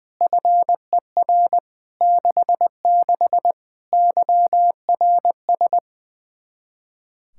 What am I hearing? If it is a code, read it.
Morse code, heard as a keyed tone, about 20 wpm, 700 Hz. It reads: FER 66 YRS
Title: K1USN SST QSO Practice 20wpm